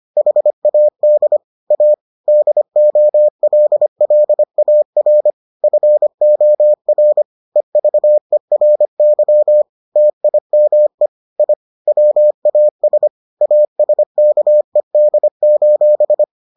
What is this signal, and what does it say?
Morse code, heard as a keyed tone, about 25 wpm, 600 Hz. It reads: HAD A DOLLAR FOR EVERY TIME I WAS ASKED: